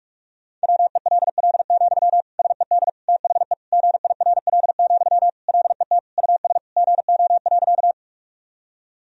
Morse code, heard as a keyed tone, about 41 wpm, 700 Hz. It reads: WELL, SED THE GIRL, LET US GO.
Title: Sentences from Top 400 Words and Abbreviations 45wpm